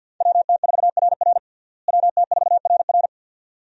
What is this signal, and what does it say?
Morse code, heard as a keyed tone, about 53 wpm, 700 Hz. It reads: WT4RR WT4RR
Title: USA Call Signs - 5 Characters - Repeated 2x - 50wpm